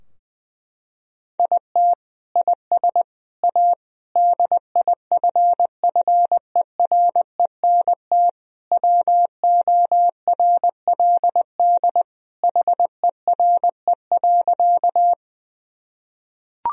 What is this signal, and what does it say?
Morse code, heard as a keyed tone, about 20 wpm, 700 Hz. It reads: IT IS A DIFFERENT WORLD HERE.
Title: Sentences from Top 1000 Words - Review Part 3 - 20wpm